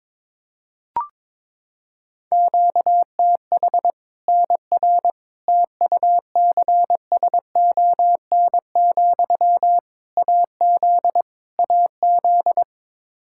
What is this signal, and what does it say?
Morse code, heard as a keyed tone, about 22 wpm, 700 Hz. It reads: QTH NR TUCSON, AZ AZ
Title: Practice QSO - Exchange 1 - Beginning - 22wpm